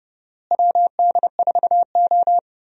Morse code, heard as a keyed tone, about 31 wpm, 700 Hz. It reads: WD4O
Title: USA Call Signs - 4 Characters - Repeated 3x - 30wpm